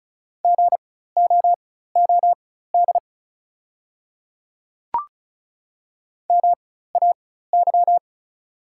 Morse code, heard as a keyed tone, about 36 wpm, spaced as for 16 wpm, 700 Hz. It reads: GOOD MAY
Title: 4x Character Spacing - Top 200 Words - 35wpm